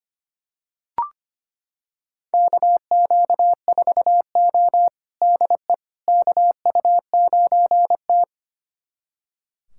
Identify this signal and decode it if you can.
Morse code, heard as a keyed tone, about 25 wpm, 700 Hz. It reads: KQ4O DE KU9T
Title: USA Callsign DE Callsign - 4 Characters - 25wpm